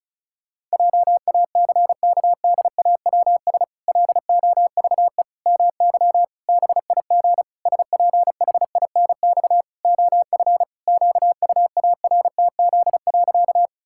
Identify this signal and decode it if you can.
Morse code, heard as a keyed tone, about 34 wpm, 700 Hz. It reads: JACKDAWS LOVE MY BIG SPHINX OF QUARTZ.
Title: Pangram Challenge v2 - 35wpm